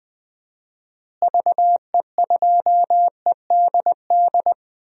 Morse code, heard as a keyed tone, about 20 wpm, 700 Hz. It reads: VE2EDD
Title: DXCC-100 Call Signs - 6 Characters - Repeated 2x - 20wpm